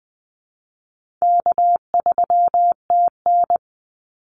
Morse code, heard as a keyed tone, about 20 wpm, 700 Hz. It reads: K3TN
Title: Easy Call Signs Repeated 2x - 20wpm